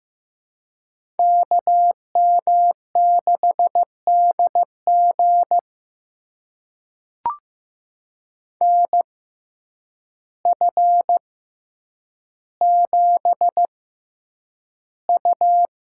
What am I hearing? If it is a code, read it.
Morse code, heard as a keyed tone, about 15 wpm, spaced as for 8 wpm, 700 Hz. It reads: KM6DG NF7U
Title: USA Call Signs - 6x Character Spacing - 15wpm